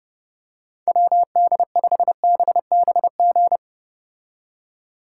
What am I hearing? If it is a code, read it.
Morse code, heard as a keyed tone, about 31 wpm, 700 Hz. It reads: WD5BBG